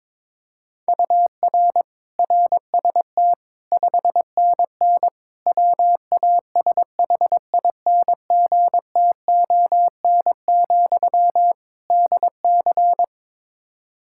Morse code, heard as a keyed tone, about 21 wpm, 700 Hz. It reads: UR RST 5NN WASHINGTON, DC
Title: Practice QSO - Exchange 1 - Beginning - 22wpm